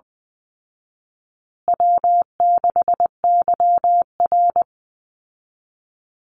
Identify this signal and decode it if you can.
Morse code, heard as a keyed tone, about 20 wpm, 700 Hz. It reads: W6YR
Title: Easy Call Signs Repeated 3x - 20wpm